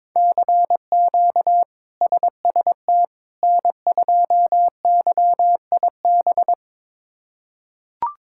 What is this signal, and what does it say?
Morse code, heard as a keyed tone, about 22 wpm, 700 Hz. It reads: CQ SST N2YIB
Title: K1USN SST QSO Practice - 22wpm